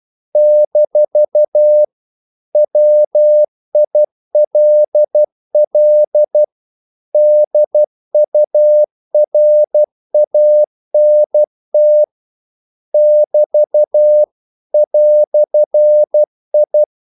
Morse code, heard as a keyed tone, about 12 wpm, 600 Hz. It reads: - WILL DURANT <BT> "I
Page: English quotations in CW (telegraphy - morse)